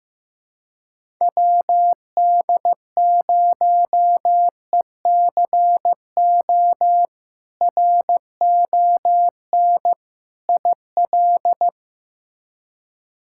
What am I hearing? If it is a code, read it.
Morse code, heard as a keyed tone, about 15 wpm, 700 Hz. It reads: WD0ECO RON IL
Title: K1USN SST QSO Practice - 15wpm